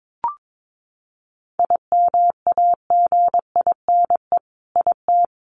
Morse code, heard as a keyed tone, about 22 wpm, 700 Hz. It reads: IMAGINE IT
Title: Sentences from Top 1400 Words 22wpm